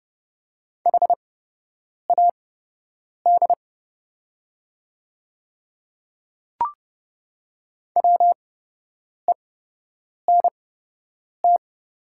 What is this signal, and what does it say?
Morse code, heard as a keyed tone, about 30 wpm, spaced as for 8 wpm, 700 Hz. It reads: HAD WENT
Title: 8x Character Spacing - Top 200 Words - 30wpm